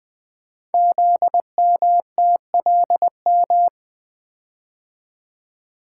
Morse code, heard as a keyed tone, about 20 wpm, 700 Hz. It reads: ZMTLM
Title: Code Groups - 5 Letters - 20wpm